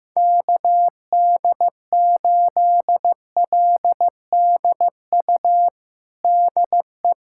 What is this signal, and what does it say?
Morse code, heard as a keyed tone, about 15 wpm, 700 Hz. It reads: KD8LDU DE
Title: USA Callsign DE Callsign - 6 Characters - 15wpm